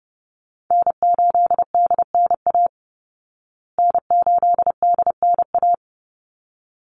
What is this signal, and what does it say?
Morse code, heard as a keyed tone, about 30 wpm, 700 Hz. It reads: N8DNA N8DNA
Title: Call Signs Repeated 3x - 30wpm